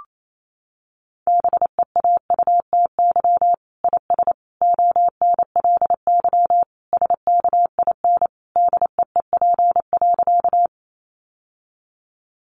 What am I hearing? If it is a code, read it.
Morse code, heard as a keyed tone, about 28 wpm, 700 Hz. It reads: BEAUTY IS ONLY SKIN DEEP.
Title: Sentences from Top 1300 Words 28wpm